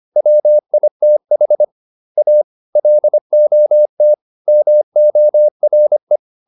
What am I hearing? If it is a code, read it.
Morse code, heard as a keyed tone, about 25 wpm, 600 Hz. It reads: WITH A LOT MORE